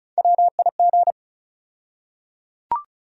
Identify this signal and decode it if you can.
Morse code, heard as a keyed tone, about 36 wpm, 700 Hz. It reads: WIG
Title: USA Call Sign Postfixes - 3 Letters 35wpm